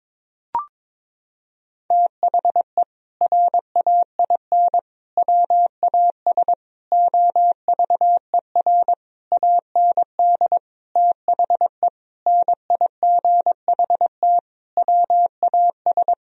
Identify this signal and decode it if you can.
Morse code, heard as a keyed tone, about 22 wpm, 700 Hz. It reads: THE RAIN WAS OVER AND THE NIGHT WAS
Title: Sentences from Top 1000 Words - Review Part 1 - 22wpm